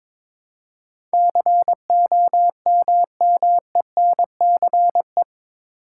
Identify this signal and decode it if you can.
Morse code, heard as a keyed tone, about 22 wpm, 700 Hz. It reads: COMMENCE
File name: practice / 22wpm / Eight Letter Words - 22wpm